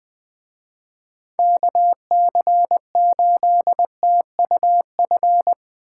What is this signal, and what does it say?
Morse code, heard as a keyed tone, about 20 wpm, 700 Hz. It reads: KC8TUF